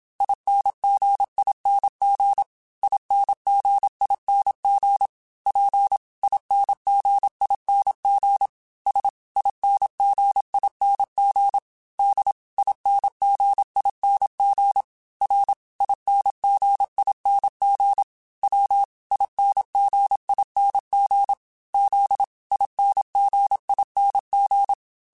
Morse code, heard as a keyed tone, about 27 wpm, 800 Hz. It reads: INGING INGING P INGING S INGING D INGING R INGING W INGING Z INGING